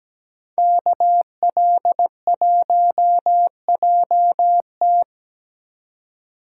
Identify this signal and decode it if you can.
Morse code, heard as a keyed tone, about 20 wpm, 700 Hz. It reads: KL1JT
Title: Speed Racing - USA 2x2 Call Signs 17wpm